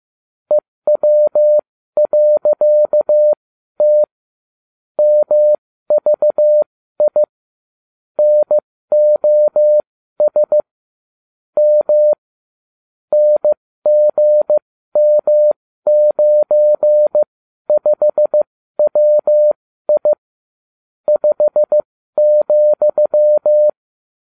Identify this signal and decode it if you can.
Morse code, heard as a keyed tone, about 15 wpm, spaced as for 12 wpm, 600 Hz. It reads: EW.T MVI NOS M NGM95WI 5,